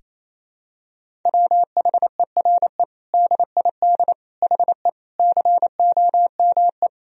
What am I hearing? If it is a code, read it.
Morse code, heard as a keyed tone, about 28 wpm, 700 Hz. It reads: WHERE DID HE COME
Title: Sentences from Top 200 Words and Abbreviations 28wpm